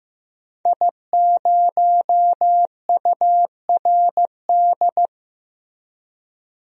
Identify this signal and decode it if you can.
Morse code, heard as a keyed tone, about 15 wpm, 700 Hz. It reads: I0URD
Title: DXCC-100 Call Signs - 5 Characters - Repeated 3x - 15wpm